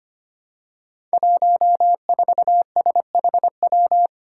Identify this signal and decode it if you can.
Morse code, heard as a keyed tone, about 25 wpm, 700 Hz. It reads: 14SHW
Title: Five Challenging Letter-Number Combinations 25wpm